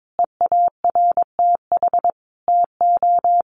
Morse code, heard as a keyed tone, about 22 wpm, 700 Hz. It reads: EARTH TO
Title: Sentences from Top 1500 Words 22wpm